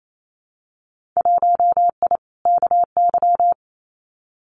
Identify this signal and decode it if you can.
Morse code, heard as a keyed tone, about 27 wpm, 700 Hz. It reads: 1I KY
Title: Winter Field Day Exchanges 28wpm